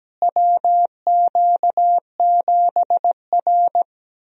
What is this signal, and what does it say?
Morse code, heard as a keyed tone, about 17 wpm, 700 Hz. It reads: WQ7R
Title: USA Call Signs - 4 Characters - Repeated 3x - 17wpm